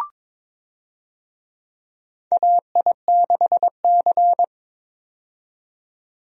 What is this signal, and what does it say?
Morse code, heard as a keyed tone, about 22 wpm, 700 Hz. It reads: AI6C
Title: USA Call Signs - 4 Characters - 22wpm